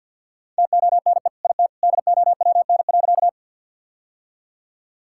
Morse code, heard as a keyed tone, about 50 wpm, 700 Hz. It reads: TONE IT DOWN.
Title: Sentences from Top 1600 Words 50wpm